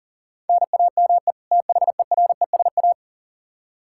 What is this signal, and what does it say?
Morse code, heard as a keyed tone, about 40 wpm, 700 Hz. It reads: NAME THERESA
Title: K1USN SST QSO Practice 40wpm